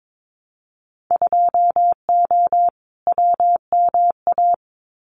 Morse code, heard as a keyed tone, about 22 wpm, 700 Hz. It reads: 2O WMA
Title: Winter Field Day Exchanges Encore 22wpm